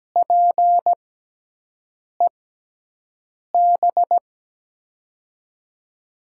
Morse code, heard as a keyed tone, about 17 wpm, spaced as for 6 wpm, 700 Hz. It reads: PEB
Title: USA Call Signs - 6 Characters - 6x Character Spacing - 17wpm